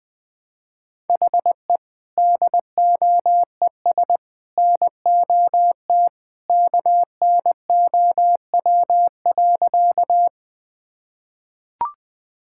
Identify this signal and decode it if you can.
Morse code, heard as a keyed tone, about 21 wpm, 700 Hz. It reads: HE DOES NOT KNOW.
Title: Sentences with 3-Word Phrases - 20wpm